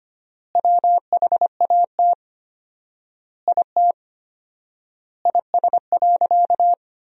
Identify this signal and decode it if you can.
Morse code, heard as a keyed tone, about 25 wpm, 700 Hz. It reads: WHAT IT IS.
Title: 4x Word-Spacing - Sentences from Top 300 Words - 25wpm